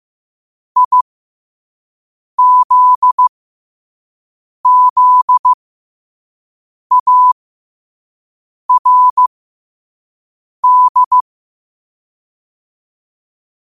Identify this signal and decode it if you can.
Morse code, heard as a keyed tone, about 15 wpm, spaced as for 5 wpm, 1000 Hz. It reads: IZZARD
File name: qotdfiles / 5wpm